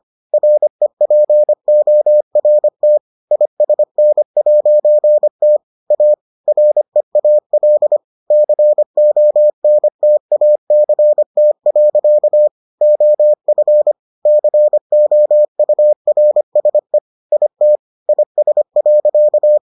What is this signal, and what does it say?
Morse code, heard as a keyed tone, about 25 wpm, 600 Hz. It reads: REPORT ISN'T A REAL CONTACT. OF COURSE IT IS.